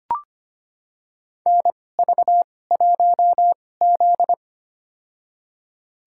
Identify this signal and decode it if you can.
Morse code, heard as a keyed tone, about 25 wpm, spaced as for 18 wpm, 700 Hz. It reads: NV1Z
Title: USA Call Signs - 4 Characters - 2x Character Spacing - 25wpm